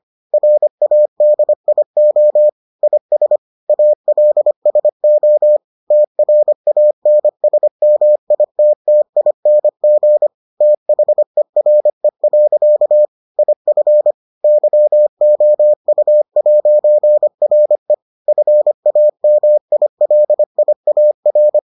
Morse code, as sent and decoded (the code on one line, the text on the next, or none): .-. .- -.. .. --- / .. ... / .- .-.. ... --- / - .-. .- -. ... -- .. - - .. -. --. / - .... .-. .-.-.- / .. ..-. / -.-- --- ..- .----. .-. / ..-. .- -- .. .-.. .. .- .-.
RADIO IS ALSO TRANSMITTING THERE. IF YOU'RE FAMILIAR